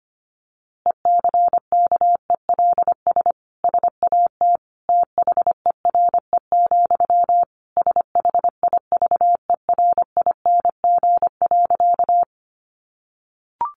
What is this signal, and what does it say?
Morse code, heard as a keyed tone, about 25 wpm, 700 Hz. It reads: ECKELS SAT THERE, SHIVERING.
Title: A Sound Of Thunder 25wpm